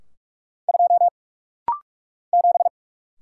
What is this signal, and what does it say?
Morse code, heard as a keyed tone, about 46 wpm, 700 Hz. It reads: 2 7
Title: Single Letter-Number - Rapid-Fire 45wpm